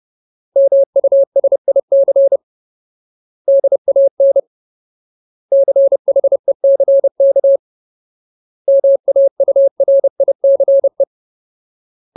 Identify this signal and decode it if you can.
Morse code, heard as a keyed tone, about 30 wpm, 550 Hz. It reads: MUSIC DAN CHECK MAURICE
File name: Ultimate Word List / 30WPM Ultimate